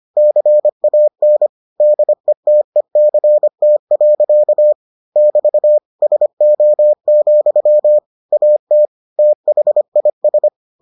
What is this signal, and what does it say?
Morse code, heard as a keyed tone, about 25 wpm, 600 Hz. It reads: CAN DETECT. <BT> SO, AT THIS